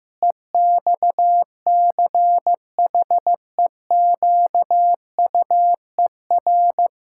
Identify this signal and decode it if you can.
Morse code, heard as a keyed tone, about 15 wpm, 700 Hz. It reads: EXCHEQUER
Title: Pangram Challenge v2 - 15wpm